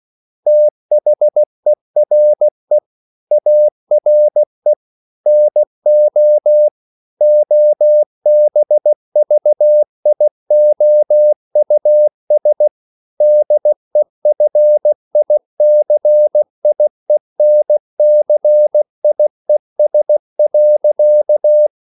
Morse code, heard as a keyed tone, about 16 wpm, 600 Hz. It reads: THERE ARE NO OBVIOUS DEFICIENCIES.
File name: CW / Practice / EnglishQuotations0216